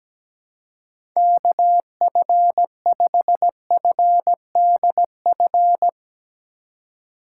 Morse code, heard as a keyed tone, about 17 wpm, 700 Hz. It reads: KF5FDF
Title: USA Call Signs - 2x Character Spacing - 17wpm